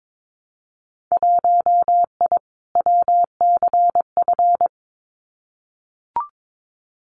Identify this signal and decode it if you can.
Morse code, heard as a keyed tone, about 22 wpm, 700 Hz. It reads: 1I WCF
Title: Winter Field Day Exchanges Encore 22wpm